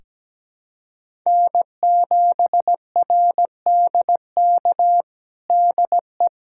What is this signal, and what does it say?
Morse code, heard as a keyed tone, about 17 wpm, 700 Hz. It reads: N7RDK DE
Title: USA Callsign DE Callsign - 5 Characters - 17wpm